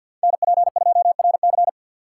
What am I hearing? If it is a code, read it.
Morse code, heard as a keyed tone, about 52 wpm, 700 Hz. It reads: NP2RC
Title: USA Call Signs - 5 Characters - 6x Character Spacing - 50wpm